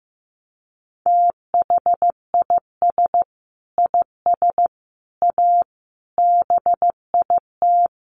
Morse code, heard as a keyed tone, about 15 wpm, 700 Hz. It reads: THIS IS A BIT
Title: Sentences from Top 900 Words 15wpm